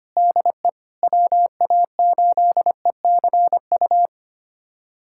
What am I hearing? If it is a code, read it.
Morse code, heard as a keyed tone, about 25 wpm, 700 Hz. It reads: DE WA8ECU
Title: USA Callsign DE Callsign - 6 Characters - 25wpm